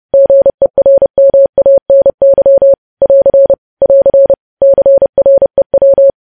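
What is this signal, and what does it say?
Morse code, heard as a keyed tone, about 30 wpm, 550 Hz. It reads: GERMANY <AR> <AR> CREW
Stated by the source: Morse news headlines sent at 30.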